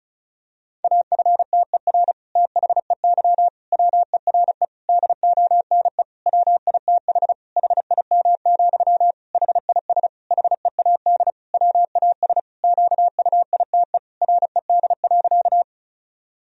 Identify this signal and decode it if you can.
Morse code, heard as a keyed tone, about 36 wpm, 700 Hz. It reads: AFTER THEY WERE DONE WITH HIM, HIS HEAD WAS QUITE RED.
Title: Sentences from Top 1000 Words - Review Part 2 - 35wpm